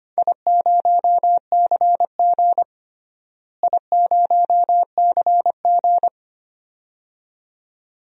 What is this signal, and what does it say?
Morse code, heard as a keyed tone, about 25 wpm, 700 Hz. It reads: I0CG I0CG
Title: DXCC-100 Call Signs - 4 Characters - Repeated 2x - 25wpm